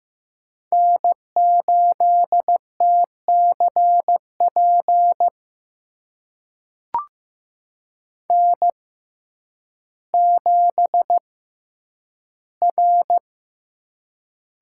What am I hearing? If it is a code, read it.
Morse code, heard as a keyed tone, about 15 wpm, spaced as for 9 wpm, 700 Hz. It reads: N8TCP N7R
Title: USA Call Signs - 5 Characters - 6x Character Spacing - 15wpm